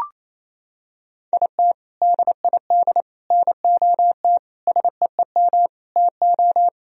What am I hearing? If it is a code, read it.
Morse code, heard as a keyed tone, about 27 wpm, 700 Hz. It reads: IT DID NOT SEEM TO
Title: Sentences from Top 1000 Words - Review Part 1 - 28wpm